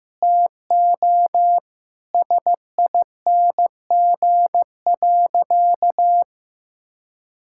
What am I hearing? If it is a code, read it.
Morse code, heard as a keyed tone, about 16 wpm, 700 Hz. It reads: TO SING.
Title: Sentences from Top 1600 Words 15wpm